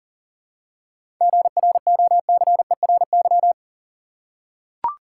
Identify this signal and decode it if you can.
Morse code, heard as a keyed tone, about 40 wpm, 700 Hz. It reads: GROCERY
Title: Seven Letter Words - 40wpm